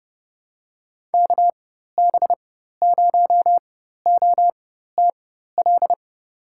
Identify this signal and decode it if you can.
Morse code, heard as a keyed tone, about 30 wpm, spaced as for 14 wpm, 700 Hz. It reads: KB0OTL
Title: USA Call Signs - 4x Character Spacing - 30wpm